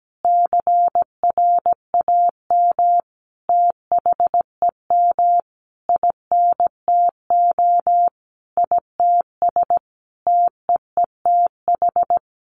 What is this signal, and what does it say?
Morse code, heard as a keyed tone, about 17 wpm, 700 Hz. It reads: CRAM THEM INTO ITS TEETH
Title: A Sound Of Thunder 17wpm